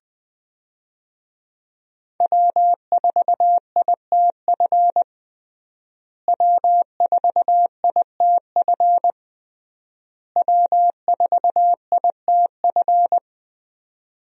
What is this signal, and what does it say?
Morse code, heard as a keyed tone, about 20 wpm, 700 Hz. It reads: W4ITF W4ITF W4ITF
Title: USA Call Signs - 5 Characters - Repeated 3x - 20wpm